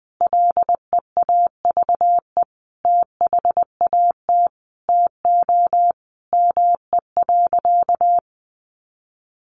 Morse code, heard as a keyed tone, about 19 wpm, 700 Hz. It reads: LEAVE THAT TO ME.
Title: Sentences from Top 400 Words 20wpm